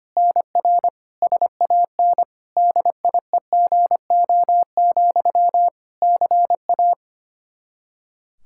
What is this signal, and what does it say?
Morse code, heard as a keyed tone, about 25 wpm, 700 Hz. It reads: NR SAN DIEGO, CA
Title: Practice QSO - Exchange 1 - Beginning - 25wpm